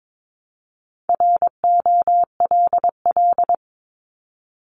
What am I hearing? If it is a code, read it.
Morse code, heard as a keyed tone, about 22 wpm, 700 Hz. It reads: ROLL
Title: Multi-letter Phonemes 22wpm